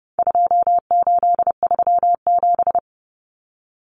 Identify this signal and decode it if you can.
Morse code, heard as a keyed tone, about 30 wpm, 700 Hz. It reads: 2837
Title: Common Cut Numbers - 4-Digit 30wpm